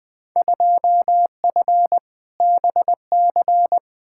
Morse code, heard as a keyed tone, about 20 wpm, 700 Hz. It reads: 2F BC
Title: Field Day Exchanges Encore - 20wpm